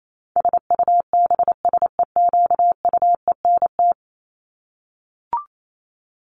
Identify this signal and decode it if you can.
Morse code, heard as a keyed tone, about 28 wpm, 700 Hz. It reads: SUBSEQUENT
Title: LY Suffix - Top 100 Words with Base Words 28wpm